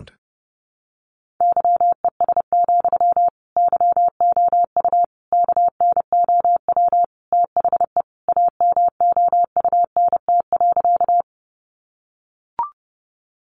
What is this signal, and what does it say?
Morse code, heard as a keyed tone, about 30 wpm, 700 Hz. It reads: YES, YOU KNOW THE AMOUNT.
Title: Sentences from Top 700 Words 30wpm